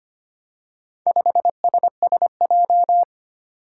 Morse code, heard as a keyed tone, about 25 wpm, 700 Hz. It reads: 5SSJ
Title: Four Challenging Letter-Number Combinations 25wpm